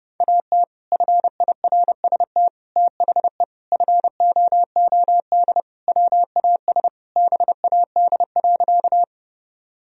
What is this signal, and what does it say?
Morse code, heard as a keyed tone, about 32 wpm, 700 Hz. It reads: AT FIRST THE FOOD WAS BAD.
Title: Sentences from Top 1000 Words - Review Part 1 - 30wpm